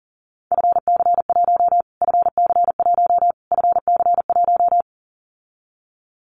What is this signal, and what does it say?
Morse code, heard as a keyed tone, about 40 wpm, 700 Hz. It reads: FC1 FC1 FC1
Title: Callsign Prefixes for Top 25 DXCC - Repeated 3x - 40wpm